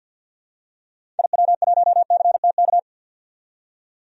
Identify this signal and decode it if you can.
Morse code, heard as a keyed tone, about 47 wpm, 700 Hz. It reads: IW1CTK
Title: DXCC-100 Call Signs - 6 Characters - 50wpm